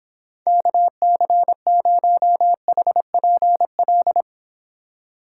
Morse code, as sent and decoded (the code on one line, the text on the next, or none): -.- -.-. ----- .... .--. .-..
KC0HPL